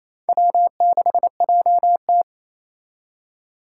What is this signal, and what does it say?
Morse code, heard as a keyed tone, about 35 wpm, 700 Hz. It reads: W6JT
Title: Speed Racing - USA 1x2 Call Signs 28wpm